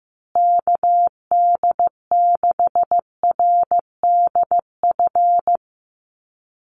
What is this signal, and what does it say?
Morse code, heard as a keyed tone, about 15 wpm, 700 Hz. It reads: KD6RDF
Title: USA Call Signs Repeated 3x 15wpm